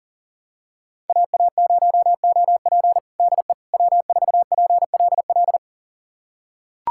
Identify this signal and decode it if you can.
Morse code, heard as a keyed tone, about 37 wpm, 700 Hz. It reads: AA0OP DE W4PLL
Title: USA Callsign DE Callsign - 5 Characters - 40wpm